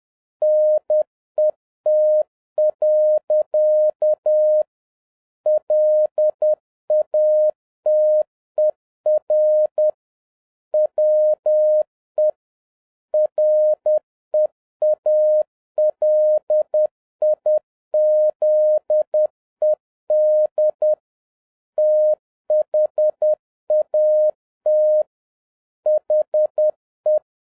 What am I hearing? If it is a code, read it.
Morse code, heard as a keyed tone, about 10 wpm, 610 Hz. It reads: NET. LATER WE REALIZED THAT HE
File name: 2011-07-mp3 / n36 THE 3.11 2 10wpm